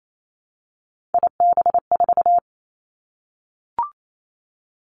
Morse code, heard as a keyed tone, about 28 wpm, 700 Hz. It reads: IB4
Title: Three Challenging Letter-Number Combinations 28wpm